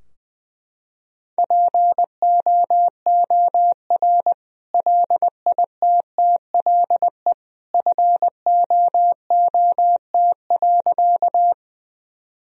Morse code, heard as a keyed tone, about 20 wpm, 700 Hz. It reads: POOR LITTLE FOOT.
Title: Sentences from Top 1000 Words - Review Part 3 - 20wpm